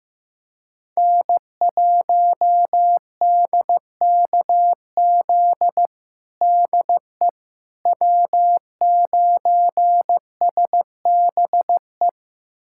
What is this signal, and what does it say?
Morse code, heard as a keyed tone, about 16 wpm, 700 Hz. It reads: N1DKZ DE W9SBE
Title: USA Callsign DE Callsign - 5 Characters - 15wpm